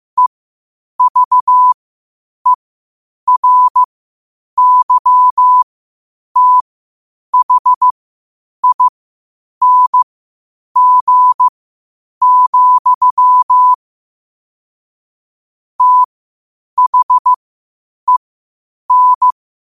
Morse code, heard as a keyed tone, about 15 wpm, spaced as for 8 wpm, 1000 Hz. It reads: EVERYTHING, THEN
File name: qotdfiles / 8wpm